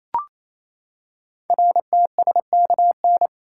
Morse code, heard as a keyed tone, about 29 wpm, 700 Hz. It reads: RTSKN
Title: Code Groups - 5 Letters - 28wpm